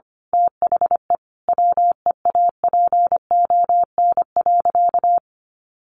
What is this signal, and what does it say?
Morse code, heard as a keyed tone, about 25 wpm, 700 Hz. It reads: THE WEAPON.
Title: A Sound Of Thunder 25wpm